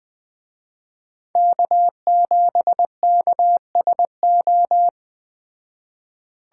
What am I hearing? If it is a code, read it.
Morse code, heard as a keyed tone, about 20 wpm, 700 Hz. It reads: K7KSO
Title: Speed Racing - USA 1x3 Call Signs 17wpm